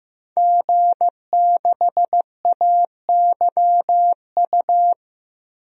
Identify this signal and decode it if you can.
Morse code, heard as a keyed tone, about 15 wpm, 700 Hz. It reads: G6AYU
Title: DXCC-100 Call Signs - 5 Characters - Repeated 3x - 15wpm